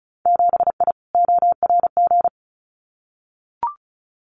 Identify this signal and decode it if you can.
Morse code, heard as a keyed tone, about 34 wpm, 700 Hz. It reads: 7I ORG
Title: Winter Field Day Exchanges Encore 35wpm